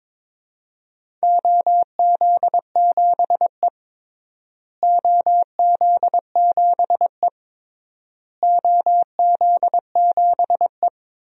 Morse code, heard as a keyed tone, about 22 wpm, 700 Hz. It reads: OZ7E OZ7E OZ7E
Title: DXCC-100 Call Signs - 4 Characters - Repeated 3x - 22wpm